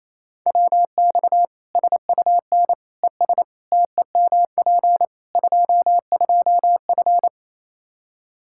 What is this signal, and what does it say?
Morse code, heard as a keyed tone, about 28 wpm, 700 Hz. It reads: WX SUN ES TEMP 22F
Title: Practice QSO - Exchange 2 - Info - 28wpm